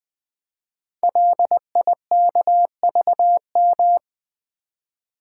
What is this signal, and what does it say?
Morse code, heard as a keyed tone, about 20 wpm, 700 Hz. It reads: LIKVM
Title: Code Groups - 5 Letters - 20wpm